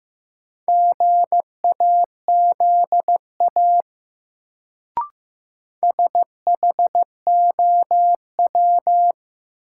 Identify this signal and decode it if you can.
Morse code, heard as a keyed tone, about 15 wpm, 700 Hz. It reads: GAZA SHOW
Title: Sets of 5 Words - New Word Drill - 15wpm